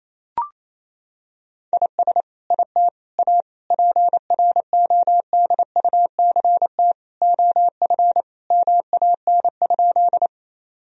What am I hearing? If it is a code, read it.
Morse code, heard as a keyed tone, about 28 wpm, 700 Hz. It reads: IS IT A PRODUCT OF MAN?
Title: Sentences from Top 1000 Words - Review Part 2 - 28wpm